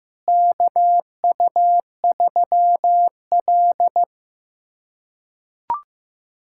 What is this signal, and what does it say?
Morse code, heard as a keyed tone, about 15 wpm, 700 Hz. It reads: KU3L
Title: USA Callsign DE Callsign - 4 Characters - 15wpm